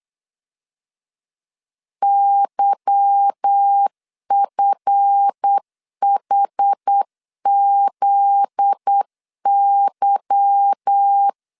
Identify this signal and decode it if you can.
Morse code, heard as a keyed tone, about 8 wpm, 800 Hz. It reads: YFHZY
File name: LEZIONI / 8 YJ45 / 8 Y 8wpm 3245